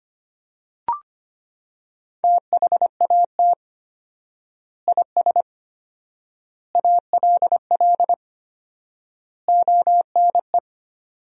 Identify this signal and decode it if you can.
Morse code, heard as a keyed tone, about 25 wpm, 700 Hz. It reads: THAT IS ALL ONE
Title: 4x Word-Spacing - Sentences from Top 200 Words - 25wpm